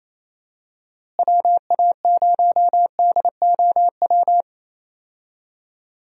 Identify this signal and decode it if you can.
Morse code, heard as a keyed tone, about 27 wpm, 700 Hz. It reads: WA0DOW
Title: USA Call Signs - 6 Characters - Repeated 2x - 28wpm